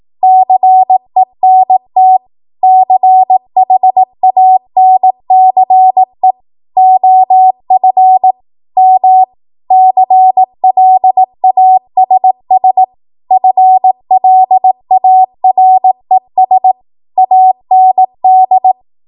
Morse code, heard as a keyed tone, about 18 wpm, 750 Hz. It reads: CENT CHANCE OF M CLASS FLARES AND
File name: Archive / 18 WPM / 18WPM